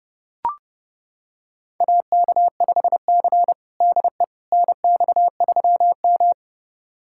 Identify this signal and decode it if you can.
Morse code, heard as a keyed tone, about 32 wpm, 700 Hz. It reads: AK5C DE NX3M
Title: USA Callsign DE Callsign - 4 Characters - 30wpm